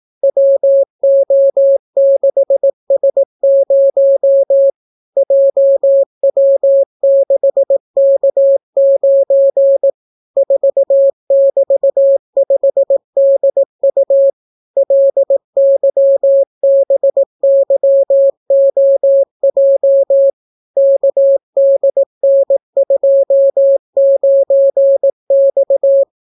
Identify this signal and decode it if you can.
Morse code, heard as a keyed tone, about 18 wpm, 550 Hz. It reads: WO6S0 JW6K9 4<BT>5DU LYBYOJ KDN29X